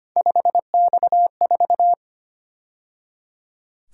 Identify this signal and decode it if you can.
Morse code, heard as a keyed tone, about 29 wpm, 700 Hz. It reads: 5X4
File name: Speed-Racing - Three Letter-Number Combinations - 25wpm